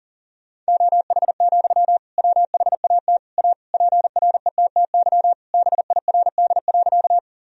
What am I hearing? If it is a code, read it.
Morse code, heard as a keyed tone, about 40 wpm, 700 Hz. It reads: OH, WHAT A PRETTY BIRD.
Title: Sentences from Top 1000 Words - Review Part 2 - 40wpm